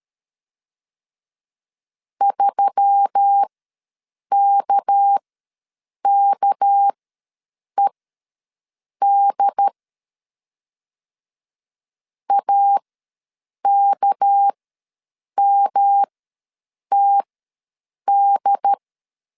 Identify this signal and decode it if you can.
Morse code, heard as a keyed tone, about 13 wpm, spaced as for 7 wpm, 800 Hz. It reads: 3KKED AKMTD
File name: LEZIONI / 6 PKX0 / 6 PK ws 12wpm 3116